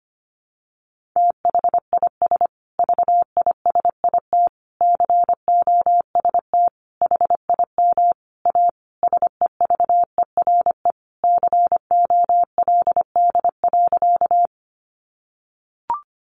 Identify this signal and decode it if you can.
Morse code, heard as a keyed tone, about 25 wpm, 700 Hz. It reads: THIS VISIT COST HIM A SEVERE COLD.
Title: Sentences from Top 1400 Words 25wpm